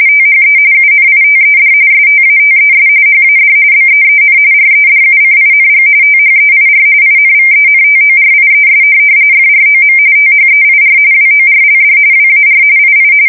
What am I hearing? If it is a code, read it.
RTTY, about 45.45 baud. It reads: THE UNITED STATES HAD BEEN REDUCING THEIR INVENTORIES FROM THEIR WORLD WAR II
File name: fsk170 / USS-Missouri